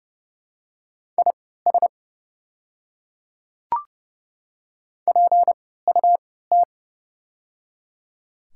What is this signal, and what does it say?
Morse code, heard as a keyed tone, about 30 wpm, spaced as for 17 wpm, 700 Hz. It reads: IS PUT
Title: 3x Character Spacing - Top 300 Words - 30wpm